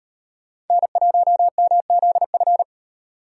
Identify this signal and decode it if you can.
Morse code, heard as a keyed tone, about 33 wpm, 700 Hz. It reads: N1MZF